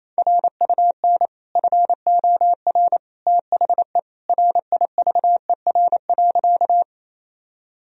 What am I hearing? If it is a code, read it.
Morse code, heard as a keyed tone, about 27 wpm, 700 Hz. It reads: RUN FOR THE RIVER.
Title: Sentences from Top 1000 Words - Review Part 1 - 28wpm